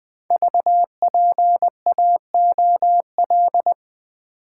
Morse code, heard as a keyed tone, about 20 wpm, 700 Hz. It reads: VPAOL
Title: Code Groups - 5 Letters - 20wpm